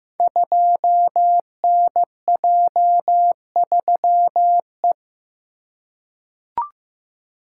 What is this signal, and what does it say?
Morse code, heard as a keyed tone, about 15 wpm, 700 Hz. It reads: 2NJ3E
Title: Code Groups - 5 Letter:Numbers - 15wpm